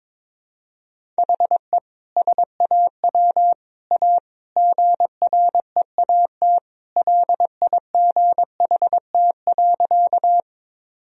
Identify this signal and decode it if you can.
Morse code, heard as a keyed tone, about 22 wpm, 700 Hz. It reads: HE SAW A GREAT LIGHT.
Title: Sentences from Top 1000 Words - Review Part 1 - 22wpm